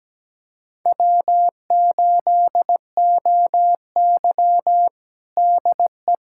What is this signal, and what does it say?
Morse code, heard as a keyed tone, about 17 wpm, 700 Hz. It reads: W8OY DE
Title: USA Callsign DE Callsign - 4 Characters - 17wpm